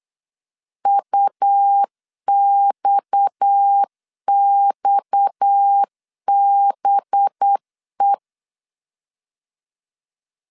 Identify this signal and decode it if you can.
Morse code, heard as a keyed tone, about 8 wpm, 800 Hz. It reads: UXXBE
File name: LEZIONI / 6 PKX0 / 6 PKX 8wpm 3041